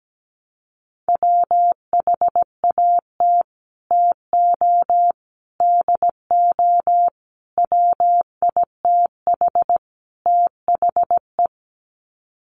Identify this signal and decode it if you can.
Morse code, heard as a keyed tone, about 17 wpm, 700 Hz. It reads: WHAT TO DO WITH THE
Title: Sets of 5 Words Encore2 17wpm